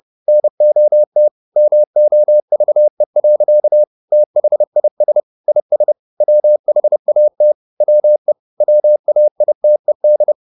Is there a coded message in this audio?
Yes: Morse code from 0.3 to 10.3 s, heard as a keyed tone, about 30 wpm, 600 Hz. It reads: NOT MOVE. THIS IS WHAT WE WAITED